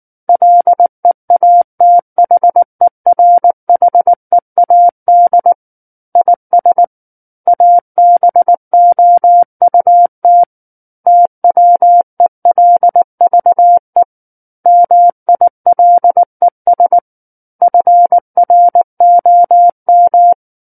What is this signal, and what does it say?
Morse code, heard as a keyed tone, about 19 wpm, 700 Hz. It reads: LEATHERHEAD IS ABOUT TWELVE MILES FROM
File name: Book One / 10-Chapter 10 - 19 WPM